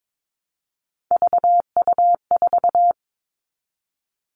Morse code, heard as a keyed tone, about 22 wpm, 700 Hz. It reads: VU4
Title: All Cut Numbers - 3-Digit 22wpm